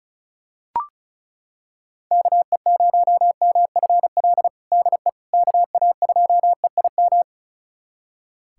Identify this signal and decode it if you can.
Morse code, heard as a keyed tone, about 36 wpm, 700 Hz. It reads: KE0MFL DE KA2EIM
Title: USA Callsign DE Callsign - 6 Characters - 35wpm